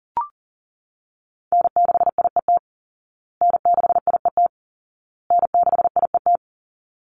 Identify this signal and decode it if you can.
Morse code, heard as a keyed tone, about 40 wpm, 700 Hz. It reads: N6IET N6IET N6IET
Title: Call Signs Repeated 2x - 40wpm